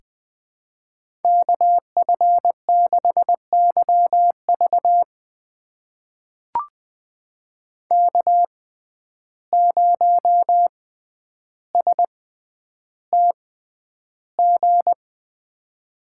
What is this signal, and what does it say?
Morse code, heard as a keyed tone, about 20 wpm, spaced as for 10 wpm, 700 Hz. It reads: KF6YV K0STG
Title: USA Call Signs - 5 Characters - 6x Character Spacing - 20wpm